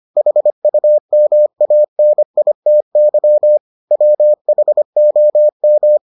Morse code, heard as a keyed tone, about 25 wpm, 600 Hz. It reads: HUMANITY WHOM